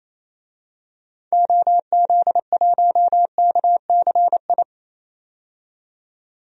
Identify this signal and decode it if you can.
Morse code, heard as a keyed tone, about 28 wpm, 700 Hz. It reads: OZ1KCI
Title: DXCC-100 Call Signs - 6 Characters - 28wpm